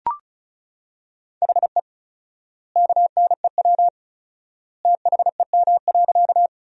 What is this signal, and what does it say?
Morse code, heard as a keyed tone, about 34 wpm, 700 Hz. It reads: HE KNEW THEM.
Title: 4x Word-Spacing - Sentences from Top 400 Words - 35wpm